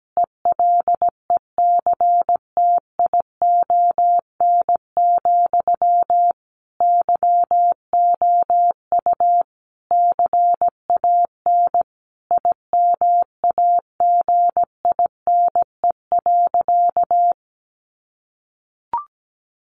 Morse code, heard as a keyed tone, about 17 wpm, 700 Hz. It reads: ELECTION, YOU CAN IMAGINE.
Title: Sentences from Top 1500 Words 17wpm